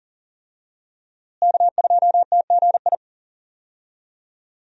Morse code, heard as a keyed tone, about 40 wpm, 700 Hz. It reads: K2TGI
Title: USA Call Signs - 5 Characters - Repeated 3x - 40wpm